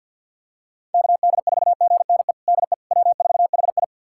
Morse code, heard as a keyed tone, about 51 wpm, 700 Hz. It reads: KD4GNE DE W4HI
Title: USA Callsign DE Callsign 50wpm